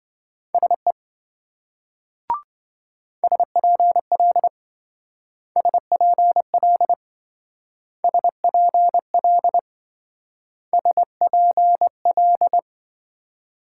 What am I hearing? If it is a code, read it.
Morse code, heard as a keyed tone, about 25 wpm, 700 Hz. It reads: SE SPL SPL SPL SPL
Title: Speed-Racing Multi-letter Phonemes - 20wpm